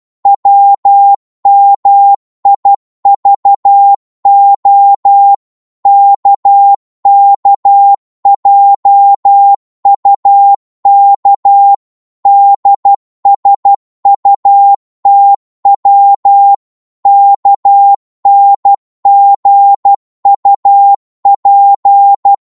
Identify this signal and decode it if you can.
Morse code, heard as a keyed tone, about 12 wpm, 800 Hz. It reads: WMIVO KKJUK DSUTW KNGUP
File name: Morse / ltr-12wpm